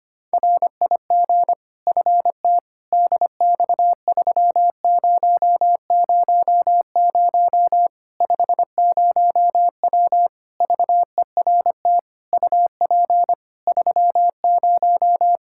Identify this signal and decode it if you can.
Morse code, heard as a keyed tone, about 25 wpm, 700 Hz. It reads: RIG FT DX3000 50W VERT UP 30
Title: Practice QSO - Exchange 2 - Info - 25wpm